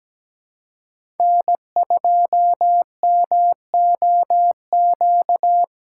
Morse code, heard as a keyed tone, about 17 wpm, 700 Hz. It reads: N2MOQ
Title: USA Call Signs - 5 Characters - 4x Character Spacing - 17wpm